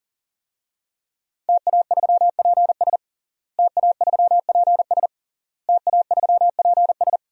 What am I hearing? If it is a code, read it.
Morse code, heard as a keyed tone, about 37 wpm, 700 Hz. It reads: TA3PS TA3PS TA3PS
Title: DXCC-100 Call Signs - 5 Characters - Repeated 3x - 40wpm